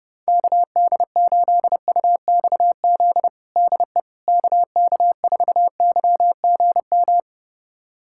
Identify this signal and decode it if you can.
Morse code, heard as a keyed tone, about 30 wpm, 700 Hz. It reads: KD8UXZ DE KK4YGM
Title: USA Callsign DE Callsign - 6 Characters - 30wpm